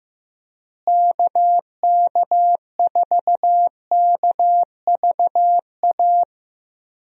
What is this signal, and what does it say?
Morse code, heard as a keyed tone, about 15 wpm, 700 Hz. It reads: KK4KVA
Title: USA Call Signs - 6 Characters - Repeated 2x - 15wpm